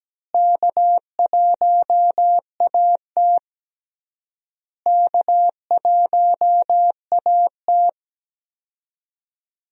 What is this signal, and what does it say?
Morse code, heard as a keyed tone, about 17 wpm, 700 Hz. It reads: K1AT K1AT
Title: USA Call Signs - 4 Characters - Repeated 3x - 17wpm